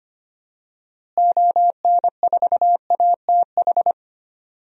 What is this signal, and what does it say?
Morse code, heard as a keyed tone, about 25 wpm, 700 Hz. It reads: ON4ATH
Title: DXCC-100 Call Signs - 6 Characters - Repeated 2x - 25wpm